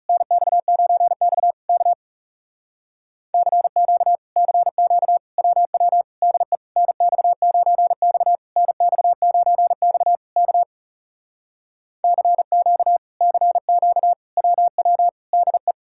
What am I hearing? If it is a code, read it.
Morse code, heard as a keyed tone, about 40 wpm, 700 Hz. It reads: NX9X K CQ CQ WW DE NX9X NX9X K CQ CQ WW DE